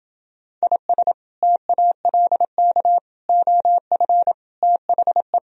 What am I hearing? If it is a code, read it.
Morse code, heard as a keyed tone, about 28 wpm, 700 Hz. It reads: IS TALK OF THE